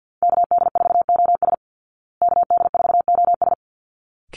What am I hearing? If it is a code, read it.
Morse code, heard as a keyed tone, about 50 wpm, 700 Hz. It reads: KD4OS KD4OS
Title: Call Signs Repeated 3x - 50wpm